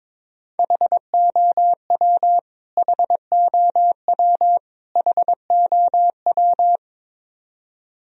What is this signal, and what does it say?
Morse code, heard as a keyed tone, about 22 wpm, 700 Hz. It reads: HOW HOW HOW
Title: Top 100 Words Repeated 22wpm